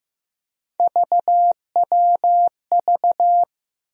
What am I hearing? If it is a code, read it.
Morse code, heard as a keyed tone, about 15 wpm, 700 Hz. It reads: VWV
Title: USA Call Sign Postfixes - 3 Letters 15wpm